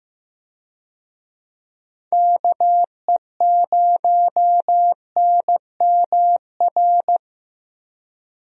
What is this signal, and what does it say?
Morse code, heard as a keyed tone, about 15 wpm, 700 Hz. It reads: KE0NMR
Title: USA Call Signs - 6 Characters - Repeated 2x - 15wpm